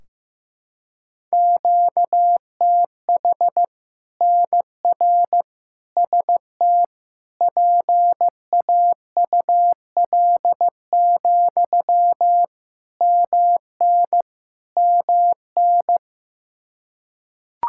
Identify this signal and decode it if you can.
Morse code, heard as a keyed tone, about 16 wpm, 700 Hz. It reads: QTH NR ST PAUL, MN MN
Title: Practice QSO - Exchange 1 - Beginning - 15wpm